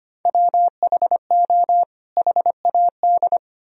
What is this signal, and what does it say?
Morse code, heard as a keyed tone, about 31 wpm, 700 Hz. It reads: WHO HAD
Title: Speed-Racing - Sets of 2 Words 25wpm